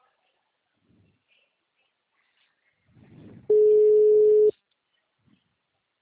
Ringback tone (Europe, 425 Hz).